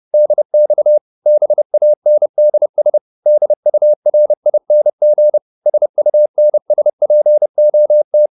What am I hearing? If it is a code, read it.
Morse code, heard as a keyed tone, about 30 wpm, 600 Hz. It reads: DX BANDS DURING SUNSPOT